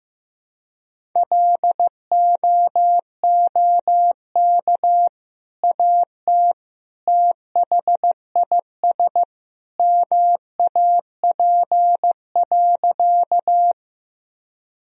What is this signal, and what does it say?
Morse code, heard as a keyed tone, about 15 wpm, 700 Hz. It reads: LOOK AT THIS MAP.
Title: Sentences from Top 1000 Words - Review Part 2 - 15wpm